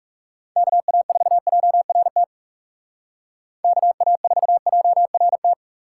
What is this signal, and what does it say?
Morse code, heard as a keyed tone, about 36 wpm, 700 Hz. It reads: KA4JRT KA4JRT
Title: Speed Racing - USA 2x3 Call Signs 30wpm